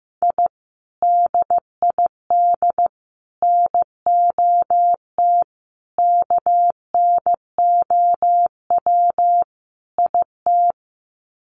Morse code, heard as a keyed tone, about 15 wpm, 700 Hz. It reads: I DID NOT KNOW IT
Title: Sets of 5 Words Encore2 15wpm